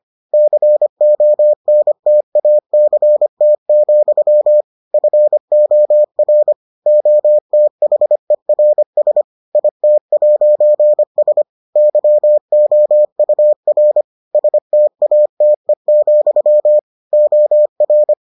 Morse code, heard as a keyed tone, about 25 wpm, 600 Hz. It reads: CONTACT, FOR OTHERS IT'S YOUR STATE, OR